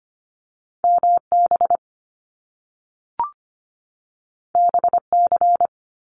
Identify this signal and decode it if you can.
Morse code, heard as a keyed tone, about 25 wpm, 700 Hz. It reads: MB BC
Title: Canadian Provinces and Territories 25wpm